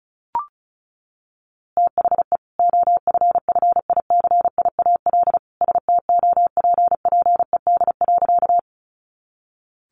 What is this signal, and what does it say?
Morse code, heard as a keyed tone, about 34 wpm, 700 Hz. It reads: THE OFFICIAL STOPPED.
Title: A Sound Of Thunder 35wpm